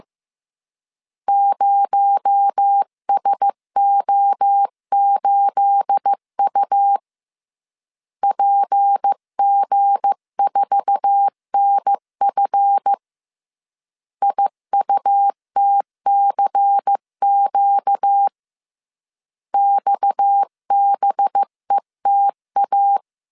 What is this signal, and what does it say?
Morse code, heard as a keyed tone, about 15 wpm, 800 Hz. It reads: 0SO8U PG4NF IUTCQ XBETA
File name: CW / DET / 1 9 1500C S B 14wpm 3302